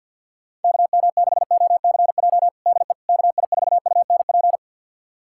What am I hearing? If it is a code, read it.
Morse code, heard as a keyed tone, about 51 wpm, 700 Hz. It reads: KM6OCJ DE KI4UNP
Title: USA Callsign DE Callsign - 6 Characters - 50wpm